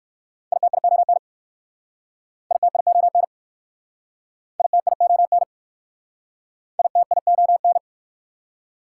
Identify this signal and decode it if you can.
Morse code, heard as a keyed tone, about 62 wpm, 690 Hz. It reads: ITION ITION ITION ITION